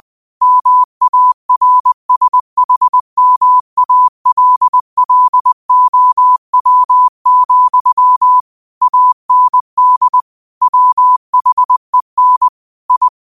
Morse code, heard as a keyed tone, about 20 wpm, 1000 Hz. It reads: MARSHMALLOW, AND WHEN I